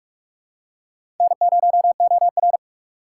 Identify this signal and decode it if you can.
Morse code, heard as a keyed tone, about 47 wpm, 700 Hz. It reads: N0OR
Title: USA Call Signs - 2x Character Spacing - 45wpm